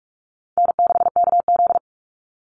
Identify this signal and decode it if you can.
Morse code, heard as a keyed tone, about 44 wpm, 690 Hz. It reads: N6KZ
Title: NA SOTA Chaser Callsigns 45wpm